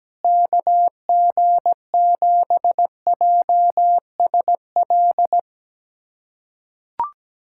Morse code, heard as a keyed tone, about 17 wpm, 700 Hz. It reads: KG7JSL
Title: USA Callsign DE Callsign - 6 Characters - 17wpm